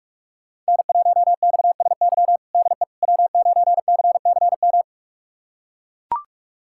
Morse code, heard as a keyed tone, about 42 wpm, 700 Hz. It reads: N1XSY DE W9CCM
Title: USA Callsign DE Callsign - 5 Characters - 45wpm